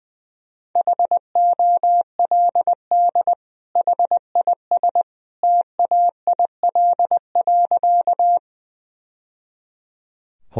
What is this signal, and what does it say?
Morse code, heard as a keyed tone, about 21 wpm, 700 Hz. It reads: HOLD HIS TAIL.
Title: Sentences from Top 1000 Words - Review Part 3 - 20wpm